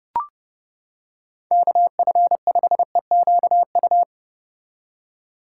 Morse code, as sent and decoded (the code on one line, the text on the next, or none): -.- ..-. ..... --.- ..-
KF5EQU